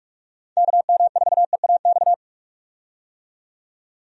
Morse code, heard as a keyed tone, about 46 wpm, 700 Hz. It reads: KM4EAX
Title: USA Call Signs - 6 Characters - 45wpm